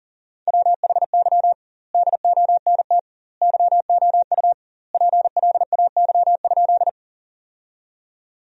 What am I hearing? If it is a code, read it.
Morse code, heard as a keyed tone, about 37 wpm, 700 Hz. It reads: WHY DONT YOU PLAY?
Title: 2x Word-Spacing - Sentences from Top 300 Words - 40wpm